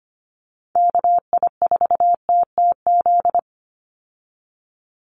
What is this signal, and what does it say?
Morse code, heard as a keyed tone, about 25 wpm, 700 Hz. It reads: KI4TTZ
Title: USA Call Signs 25wpm